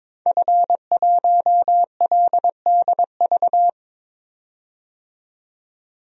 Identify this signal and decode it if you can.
Morse code, heard as a keyed tone, about 22 wpm, 700 Hz. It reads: F1LDV
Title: DXCC-100 Call Signs - 5 Characters - 22wpm